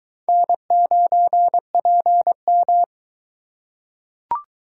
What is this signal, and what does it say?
Morse code, heard as a keyed tone, about 22 wpm, 700 Hz. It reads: N9PM